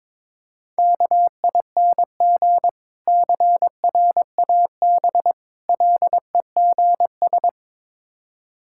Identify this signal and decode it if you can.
Morse code, heard as a keyed tone, about 22 wpm, 700 Hz. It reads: KING CRAB LEGS
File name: Difficult Words with Hints - 22wpm